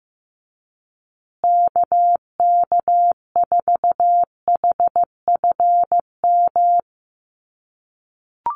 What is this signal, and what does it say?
Morse code, heard as a keyed tone, about 14 wpm, 700 Hz. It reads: KK4HFM
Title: USA Call Signs Repeated 3x 15wpm